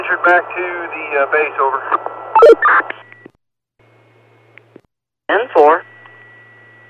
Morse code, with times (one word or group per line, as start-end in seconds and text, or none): none